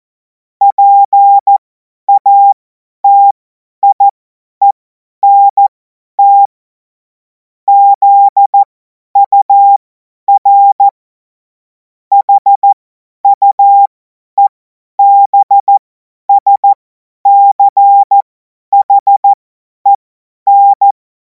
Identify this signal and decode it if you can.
Morse code, heard as a keyed tone, about 14 wpm, spaced as for 10 wpm, 800 Hz. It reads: PATIENT ZUR HUEBSCHEN
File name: Morsekurs / Witze / Witz 04. August 2026 50 F70